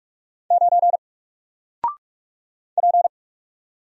Morse code, heard as a keyed tone, about 44 wpm, 700 Hz. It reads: 9 P
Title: Single Character (Letter:Number:Punctuation:Prosign) Alt 45wpm